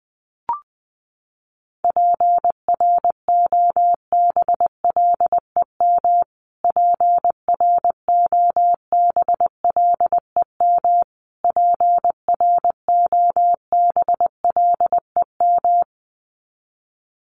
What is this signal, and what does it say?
Morse code, heard as a keyed tone, about 20 wpm, 700 Hz. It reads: PROBLEM PROBLEM PROBLEM
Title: Top 500 Words Repeated 20wpm